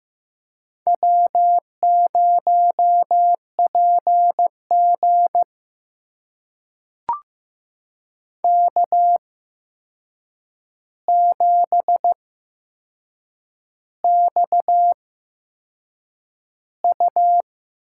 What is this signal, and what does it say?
Morse code, heard as a keyed tone, about 15 wpm, spaced as for 6 wpm, 700 Hz. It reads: W0PG K7XU
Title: USA Call Signs - 4 Characters - 8x Character Spacing - 15wpm